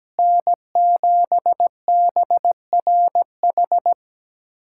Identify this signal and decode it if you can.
Morse code, heard as a keyed tone, about 17 wpm, 700 Hz. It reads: N7BRH
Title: USA Callsign DE Callsign - 5 Characters - 17wpm